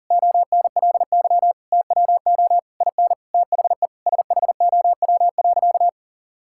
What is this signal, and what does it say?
Morse code, heard as a keyed tone, about 37 wpm, 700 Hz. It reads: ONLY TWO IN THE SHOW.
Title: Sentences from Top 1000 Words - Review Part 6 - 40wpm